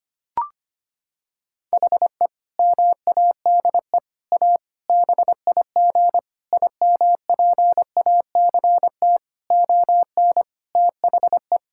Morse code, heard as a keyed tone, about 25 wpm, 700 Hz. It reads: HE MADE A BIG IMPACT ON THE
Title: Sentences from Top 1000 Words - Review Part 6 - 25wpm